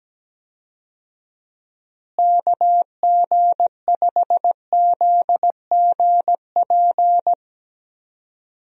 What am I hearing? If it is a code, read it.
Morse code, heard as a keyed tone, about 17 wpm, 700 Hz. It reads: KG5ZGP
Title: USA Call Signs - 6 Characters - Repeated 3x - 17wpm